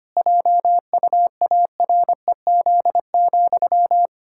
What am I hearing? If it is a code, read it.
Morse code, heard as a keyed tone, about 25 wpm, 700 Hz. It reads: JUAREZ,
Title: Pangram Challenge v2 - 25wpm